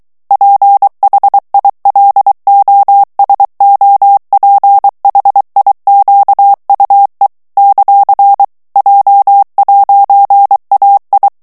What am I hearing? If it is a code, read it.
Morse code, heard as a keyed tone, about 23 wpm, 800 Hz. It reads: PHILOSOPHIQUE ; J'AI
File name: CW / FICHIERS-MP3 / FRANCAIS TEXTE CLAIR 2 VITESSE 22 MOTS MINUTES